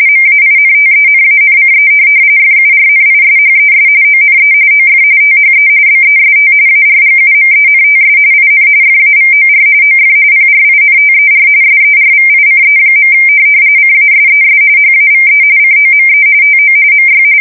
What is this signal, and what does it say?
RTTY, about 45.45 baud. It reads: MAINTENANCE FURNISHES ONE OF THE BEST ARGUMENTS IN FAVOR OF PRINTER SERVICE. CASES HAVE COME TO MY
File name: fsk170 / printing-telegraph